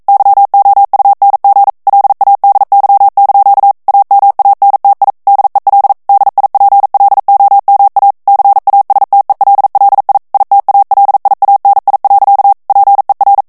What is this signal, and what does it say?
Morse code, heard as a keyed tone, about 42 wpm, 800 Hz. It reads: YOUNG LADY) AMANTI DEL DIPLOMA CASTELLI ITALIANI. PER
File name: morse mp3 / ITALIEN texte clair 1 vitesse 40 mots minute